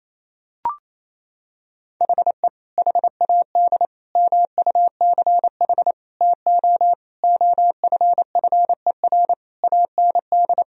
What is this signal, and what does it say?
Morse code, heard as a keyed tone, about 28 wpm, 700 Hz. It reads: HE HAD MUCH TO OFFER AND
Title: Sentences from Top 1000 Words - Review Part 4 - 28wpm